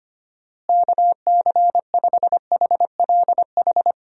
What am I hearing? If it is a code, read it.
Morse code, heard as a keyed tone, about 25 wpm, 700 Hz. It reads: KC5HLH
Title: USA Call Signs - 6 Characters - Repeated 2x - 25wpm